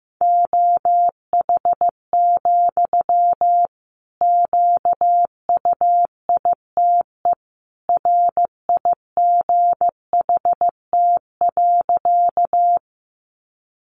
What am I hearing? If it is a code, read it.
Morse code, heard as a keyed tone, about 14 wpm, 700 Hz. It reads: OH, QUITE RIGHT.
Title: Sentences from Top 700 Words 15wpm